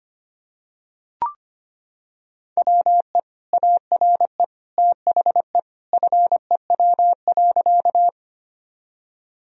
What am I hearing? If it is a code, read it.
Morse code, heard as a keyed tone, about 25 wpm, 700 Hz. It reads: WE ARE THE FEW.
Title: Sentences from Top 1000 Words - Review Part 1 - 25wpm